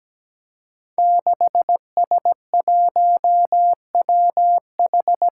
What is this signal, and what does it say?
Morse code, heard as a keyed tone, about 17 wpm, 700 Hz. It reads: 6S1WH
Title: Five Challenging Letter-Number Combinations 17wpm